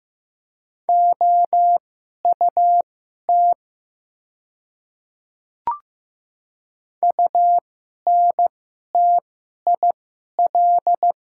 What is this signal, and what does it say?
Morse code, heard as a keyed tone, about 15 wpm, spaced as for 11 wpm, 700 Hz. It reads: OUT UNTIL
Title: 2x Character Spacing - Top 300 Words - 15wpm